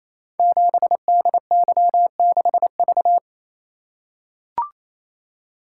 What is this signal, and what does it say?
Morse code, heard as a keyed tone, about 29 wpm, 700 Hz. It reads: 7DY6V
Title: Code Groups - 5 Letter:Numbers - 28wpm